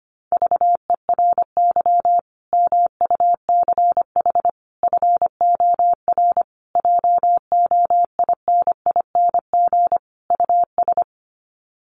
Morse code, heard as a keyed tone, about 25 wpm, 700 Hz. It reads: VERY MUCH FOR JOINING US
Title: Sets of 5 Words 25wpm